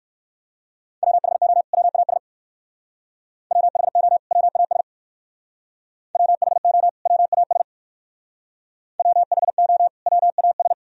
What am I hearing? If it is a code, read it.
Morse code, heard as a keyed tone, about 54 wpm, 700 Hz. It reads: WHO WAS WHO WAS WHO WAS WHO WAS
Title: Speed-Racing - Sets of 2 Words 45wpm